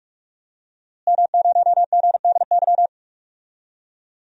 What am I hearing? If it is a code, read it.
Morse code, heard as a keyed tone, about 46 wpm, 700 Hz. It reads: M0GDY
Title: DXCC-100 Call Signs - 5 Characters - Repeated 2x - 45wpm